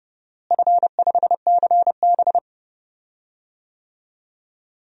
Morse code, heard as a keyed tone, about 30 wpm, 700 Hz. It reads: F5CB
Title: DXCC-100 Call Signs - 4 Characters - Repeated 2x - 30wpm